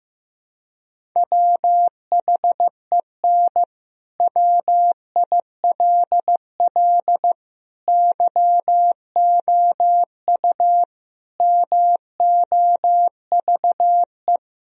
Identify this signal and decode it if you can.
Morse code, heard as a keyed tone, about 15 wpm, 700 Hz. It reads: WHEN WILL YOU MOVE
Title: Sentences from Top 1000 Words - Review Part 1 - 15wpm